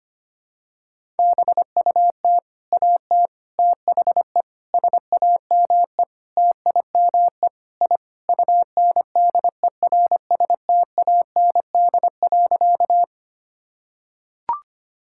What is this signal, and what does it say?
Morse code, heard as a keyed tone, about 25 wpm, 700 Hz. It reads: BUT AT THE SAME TIME I UNDERSTAND.
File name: practice / 25wpm / Sentences with 4-Word Phrases - 25wpm